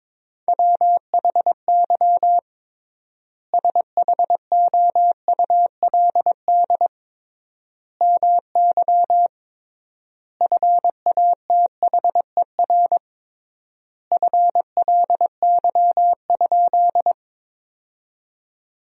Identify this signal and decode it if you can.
Morse code, heard as a keyed tone, about 22 wpm, 700 Hz. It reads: WHY SHOULD MY FATHER FLY?
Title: 3x Word-Spacing - Sentences from Top 500 Words - 22wpm